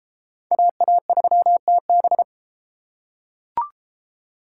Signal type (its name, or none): Morse code